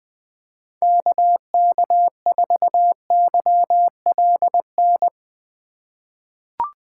Morse code, heard as a keyed tone, about 21 wpm, 700 Hz. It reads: KK4YLN
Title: USA Call Signs - 6 Characters - 8x Character Spacing - 20wpm